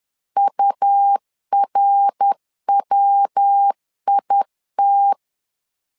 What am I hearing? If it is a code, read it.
Morse code, heard as a keyed tone, about 10 wpm, 800 Hz. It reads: URWIT
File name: LEZIONI / 3 MWRO / 3 MWR 10wpm 1952